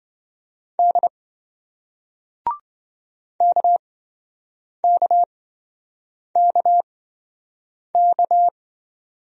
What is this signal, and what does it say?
Morse code, heard as a keyed tone, about 28 wpm, spaced as for 7 wpm, 700 Hz. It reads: D KKKK